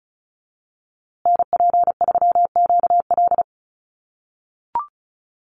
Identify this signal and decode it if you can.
Morse code, heard as a keyed tone, about 34 wpm, 700 Hz. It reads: NP3QL
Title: USA Call Signs Repeated 3x 35wpm